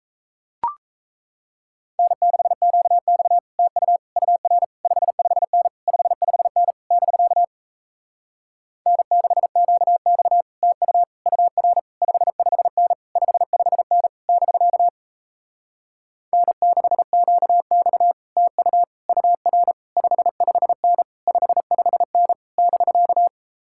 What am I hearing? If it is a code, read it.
Morse code, heard as a keyed tone, about 38 wpm, 700 Hz. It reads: N6QX TU UR 55N 55N <BK> N6QX TU UR 55N 55N <BK> N6QX TU UR 55N 55N <BK>